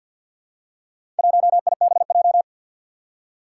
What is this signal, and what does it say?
Morse code, heard as a keyed tone, about 53 wpm, 700 Hz. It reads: 1IBJ
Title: Four Challenging Letter-Number Combinations 50wpm